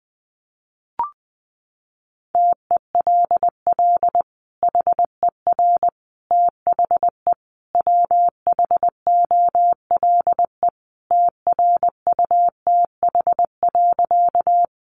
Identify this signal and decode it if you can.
Morse code, heard as a keyed tone, about 20 wpm, 700 Hz. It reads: TELL HER THE WHOLE TRUTH.
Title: Sentences from Top 800 Words 20wpm